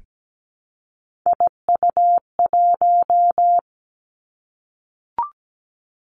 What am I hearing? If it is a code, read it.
Morse code, heard as a keyed tone, about 17 wpm, 700 Hz. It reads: IU1
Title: Three Challenging Letter-Number Combinations 17wpm